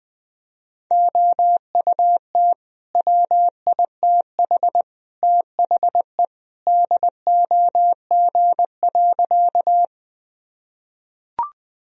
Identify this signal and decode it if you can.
Morse code, heard as a keyed tone, about 20 wpm, 700 Hz. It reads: OUT WITH THE DOG.
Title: Sentences from Top 1000 Words - Review Part 2 - 20wpm